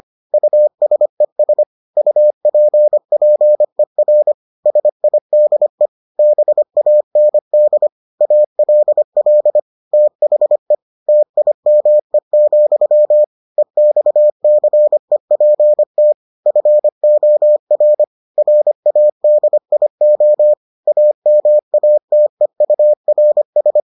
Morse code, heard as a keyed tone, about 25 wpm, 600 Hz. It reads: USES UPPER SIDE BAND ALL THE TIME, EXCEPT FOR RADIO AMATEURS